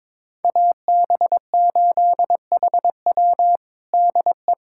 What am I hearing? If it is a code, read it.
Morse code, heard as a keyed tone, about 22 wpm, 700 Hz. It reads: AB8HW DE
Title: USA Callsign DE Callsign - 5 Characters - 22wpm